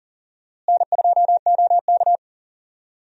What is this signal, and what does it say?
Morse code, heard as a keyed tone, about 40 wpm, 700 Hz. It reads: N2OK
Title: USA Call Signs - 4 Characters - Repeated 3x - 40wpm